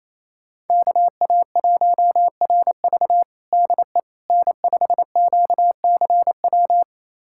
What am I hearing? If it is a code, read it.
Morse code, heard as a keyed tone, about 28 wpm, 700 Hz. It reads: KA1RV DE N5QCW
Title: USA Callsign DE Callsign - 5 Characters - 28wpm